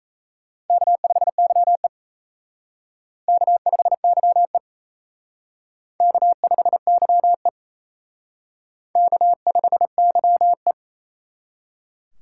Morse code, heard as a keyed tone, about 36 wpm, 700 Hz. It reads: K5YE K5YE K5YE K5YE